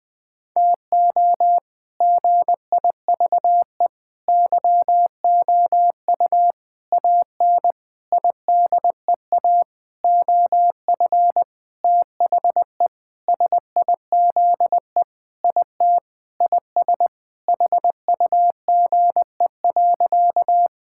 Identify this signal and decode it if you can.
Morse code, heard as a keyed tone, about 20 wpm, 700 Hz. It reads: TO GIVE YOU AN IDEA OF THE SIZE IT IS HUGE.
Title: Sentences with 6-Word Phrases - 20wpm